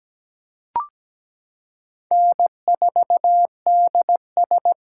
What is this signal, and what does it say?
Morse code, heard as a keyed tone, about 17 wpm, 700 Hz. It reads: N4DS
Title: USA Callsign DE Callsign - 4 Characters - 17wpm